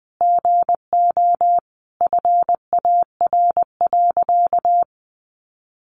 Morse code, heard as a keyed tone, about 20 wpm, 700 Hz. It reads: GO FAR.
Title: Sentences from Top 400 Words 20wpm